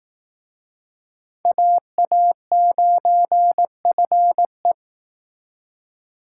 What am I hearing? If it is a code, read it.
Morse code, heard as a keyed tone, about 19 wpm, 700 Hz. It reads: AA9FE
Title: Speed Racing - USA 2x2 Call Signs 15wpm